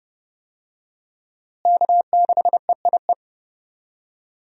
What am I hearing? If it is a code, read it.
Morse code, heard as a keyed tone, about 30 wpm, 700 Hz. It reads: K6EIE
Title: USA Call Signs - 5 Characters - Repeated 3x - 30wpm